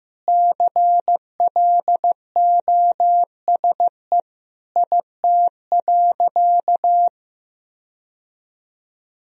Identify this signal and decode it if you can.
Morse code, heard as a keyed tone, about 15 wpm, 700 Hz. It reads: CLOSE IT.
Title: Sentences from Top 1000 Words - Review Part 1 - 15wpm